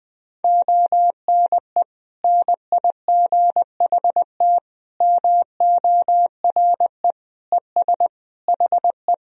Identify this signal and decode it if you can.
Morse code, heard as a keyed tone, about 19 wpm, 700 Hz. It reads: ONE NIGHT MORE ES HE
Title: Sentences from Top 300 Words and Abbreviations 20wpm